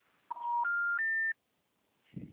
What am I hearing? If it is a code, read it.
Special information tone.